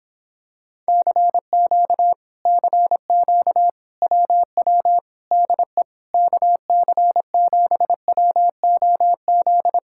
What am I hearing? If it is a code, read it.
Morse code, heard as a keyed tone, about 26 wpm, 700 Hz. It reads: CQ CQ WW DE KC7WOZ